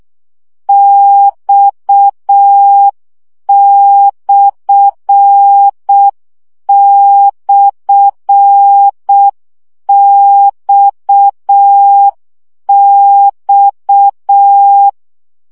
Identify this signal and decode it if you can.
Morse code, heard as a keyed tone, about 6 wpm, 800 Hz. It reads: X//XX
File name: CW gyakorlas x per 5 baud